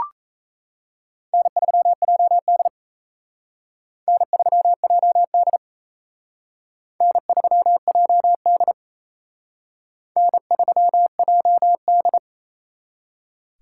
Morse code, heard as a keyed tone, about 37 wpm, 700 Hz. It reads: N3JD N3JD N3JD N3JD